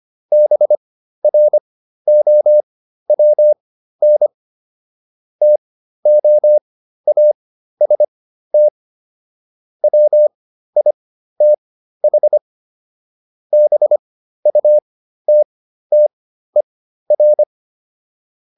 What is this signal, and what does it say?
Morse code, heard as a keyed tone, about 25 wpm, spaced as for 13 wpm, 600 Hz. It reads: BROWN TOAST WITH BUTTER